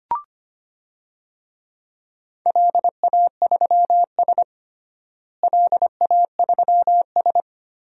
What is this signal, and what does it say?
Morse code, heard as a keyed tone, about 25 wpm, 700 Hz. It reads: LA3S LA3S
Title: DXCC-100 Call Signs - 4 Characters - Repeated 2x - 25wpm